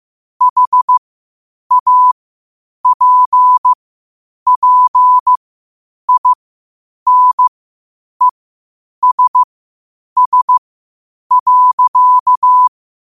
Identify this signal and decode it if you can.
Morse code, heard as a keyed tone, about 15 wpm, spaced as for 8 wpm, 1000 Hz. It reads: HAPPINESS.
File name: qotdfiles / 8wpm